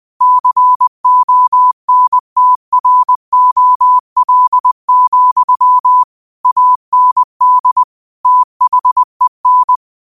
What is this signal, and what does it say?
Morse code, heard as a keyed tone, about 20 wpm, 1000 Hz. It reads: CONTROL, AND THEN